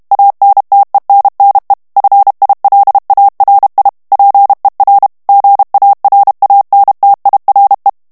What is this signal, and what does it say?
Morse code, heard as a keyed tone, about 32 wpm, 800 Hz. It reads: ANTENNE FILARI PER GARANTIRE
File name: ITALIEN texte clair 1 vitesse 30 mots minute